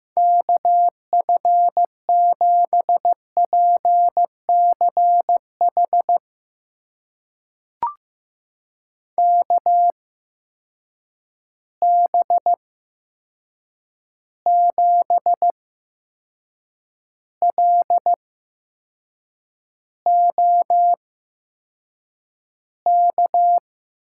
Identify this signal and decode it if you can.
Morse code, heard as a keyed tone, about 15 wpm, spaced as for 6 wpm, 700 Hz. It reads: KF7PCH KB7LOK
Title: USA Call Signs - 6 Characters - 8x Character Spacing - 15wpm